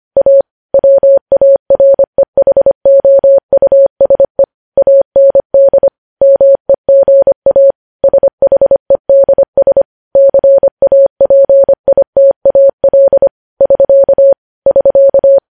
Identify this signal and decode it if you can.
Morse code, heard as a keyed tone, about 25 wpm, 550 Hz. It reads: A WAREHOUSE AND MEGA SHEDS CAPITAL <SK> <SK>